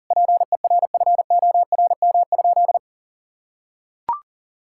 Morse code, heard as a keyed tone, about 37 wpm, 700 Hz. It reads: PERFORM?
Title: Sentences from Top 1000 Words - Review Part 6 - 40wpm